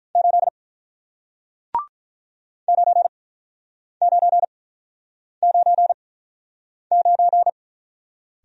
Morse code, heard as a keyed tone, about 47 wpm, spaced as for 9 wpm, 700 Hz. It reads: 8 9999